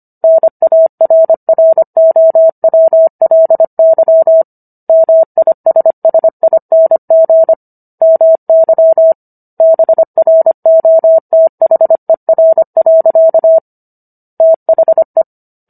Morse code, heard as a keyed tone, about 25 wpm, 650 Hz. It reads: NARROWLY MISSING MY BROTHER. THE